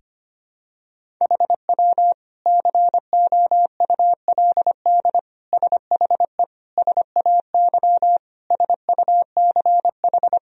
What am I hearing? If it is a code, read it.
Morse code, heard as a keyed tone, about 25 wpm, 700 Hz. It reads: HW COULD SHE SAY SUCH
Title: Sentences from Top 200 Words and Abbreviations 25wpm